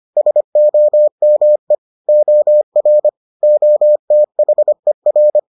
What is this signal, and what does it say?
Morse code, heard as a keyed tone, about 25 wpm, 600 Hz. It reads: SOME OR OTHER